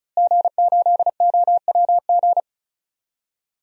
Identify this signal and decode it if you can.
Morse code, heard as a keyed tone, about 36 wpm, 700 Hz. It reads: G8OWG
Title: DXCC-100 Call Signs - 5 Characters - Repeated 3x - 35wpm